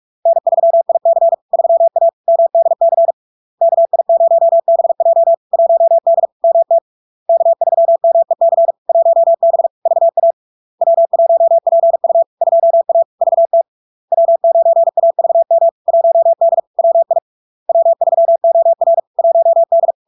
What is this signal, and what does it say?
Morse code, heard as a keyed tone, about 46 wpm, 670 Hz. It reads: N3IC 3A MDC KI0BJ 1D MT K3MEC 1B VA W1PU 2A VT W9AVM 1D WI W3OR 1D
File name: LARK-FD-2022-Log-in-CW-45wpm-670-Hz